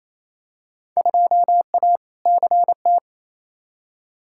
Morse code, heard as a keyed tone, about 28 wpm, 700 Hz. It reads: 2A CT
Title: Field Day Exchanges Encore - 28wpm